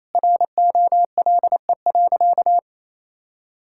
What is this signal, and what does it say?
Morse code, heard as a keyed tone, about 28 wpm, 700 Hz. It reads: ROLE.
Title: Sentences from Top 1000 Words - Review Part 6 - 28wpm